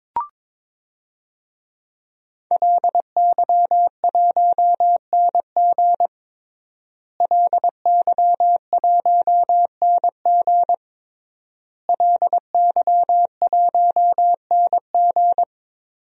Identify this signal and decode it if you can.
Morse code, heard as a keyed tone, about 22 wpm, 700 Hz. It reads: LY1NG LY1NG LY1NG
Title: DXCC-100 Call Signs - 5 Characters - Repeated 3x - 22wpm